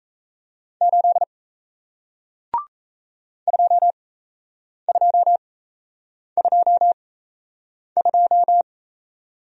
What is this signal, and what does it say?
Morse code, heard as a keyed tone, about 37 wpm, spaced as for 8 wpm, 700 Hz. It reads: 8 2222